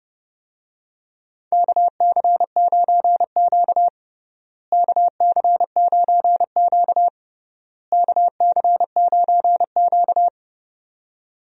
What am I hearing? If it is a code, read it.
Morse code, heard as a keyed tone, about 32 wpm, 700 Hz. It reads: KC9Q KC9Q KC9Q
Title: USA Call Signs - 4 Characters - Repeated 3x - 30wpm